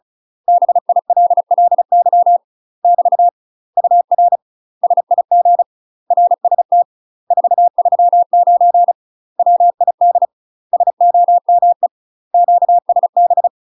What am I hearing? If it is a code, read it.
Morse code, heard as a keyed tone, about 36 wpm, 700 Hz. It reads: BILLY <BT> UR SIG RST 439 WID SOME QSB
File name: UK MASTER QSOs TRAINING / 35WPM / UK MASTER QSOs 0005